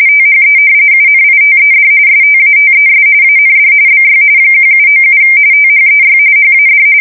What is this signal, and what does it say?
RTTY, about 45.45 baud. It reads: THE UNITED STATES AND BY 1797 THEY HAD
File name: fsk170 / usnavy1